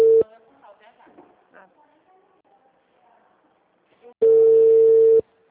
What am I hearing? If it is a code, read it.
Ringback tone (China, 450 Hz), with speech.